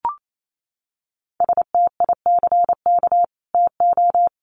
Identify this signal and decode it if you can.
Morse code, heard as a keyed tone, about 28 wpm, 700 Hz. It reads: STICK TO
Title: Sentences from Top 1200 Words 28wpm